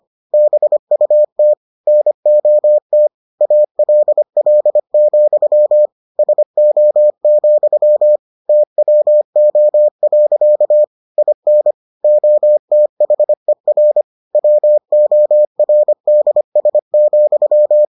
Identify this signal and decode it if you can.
Morse code, heard as a keyed tone, about 25 wpm, 600 Hz. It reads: BUT NOT ALL, SO, TWO. IN OTHER WORDS,